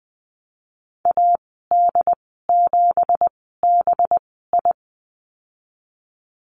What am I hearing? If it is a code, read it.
Morse code, heard as a keyed tone, about 19 wpm, spaced as for 14 wpm, 700 Hz. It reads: AD7BI
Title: 186 Lesson 30g - Letter b - Call Signs 20wpm